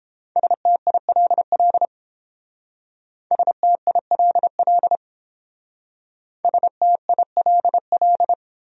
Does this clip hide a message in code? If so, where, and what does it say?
Yes: Morse code from 0.4 to 8.3 s, heard as a keyed tone, about 30 wpm, 700 Hz. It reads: STILL STILL STILL